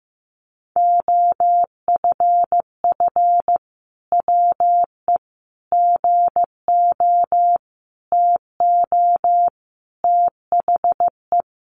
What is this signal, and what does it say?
Morse code, heard as a keyed tone, about 15 wpm, 700 Hz. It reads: OFF WE GO TO THE
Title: Sentences from Top 1400 Words 15wpm